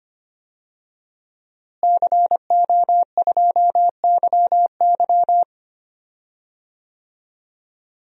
Morse code, heard as a keyed tone, about 25 wpm, 700 Hz. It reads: CO2YY
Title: DXCC-100 Call Signs - 5 Characters - Repeated 3x - 25wpm